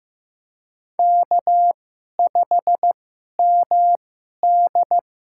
Morse code, heard as a keyed tone, about 15 wpm, spaced as for 11 wpm, 700 Hz. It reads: K5MD
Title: USA Call Signs - 4 Characters - 2x Character Spacing - 15wpm